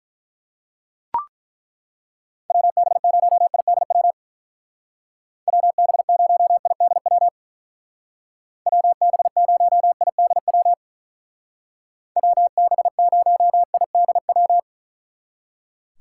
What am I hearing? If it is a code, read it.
Morse code, heard as a keyed tone, about 43 wpm, 700 Hz. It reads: WB0IDW WB0IDW WB0IDW WB0IDW
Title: Speed Racing - USA 2x3 Call Signs 35wpm